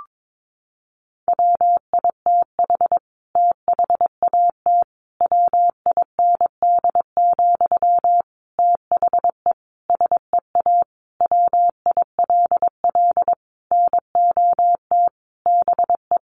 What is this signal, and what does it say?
Morse code, heard as a keyed tone, about 22 wpm, 700 Hz. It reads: WITH THAT WIND, THE SEA WILL NOT BE
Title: Sentences from Top 1400 Words 22wpm